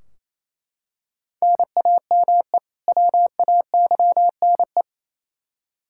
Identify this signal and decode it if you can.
Morse code, heard as a keyed tone, about 28 wpm, 700 Hz. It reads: NAME WAYNE
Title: Practice QSO - Exchange 1 - Beginning - 28wpm